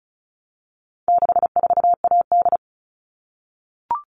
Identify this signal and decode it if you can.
Morse code, heard as a keyed tone, about 34 wpm, 700 Hz. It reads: 64AD
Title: All Cut Numbers - 4-Digit 35wpm